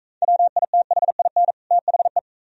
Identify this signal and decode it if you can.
Morse code, heard as a keyed tone, about 43 wpm, 690 Hz. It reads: WITHIN THE